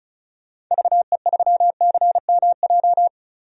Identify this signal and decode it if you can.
Morse code, heard as a keyed tone, about 36 wpm, 700 Hz. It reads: VE3CMJ
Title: DXCC-100 Call Signs - 6 Characters - 35wpm